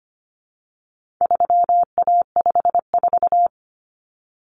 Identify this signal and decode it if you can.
Morse code, heard as a keyed tone, about 25 wpm, 700 Hz. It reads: 3A54
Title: Common Cut Numbers - 4-Digit 25wpm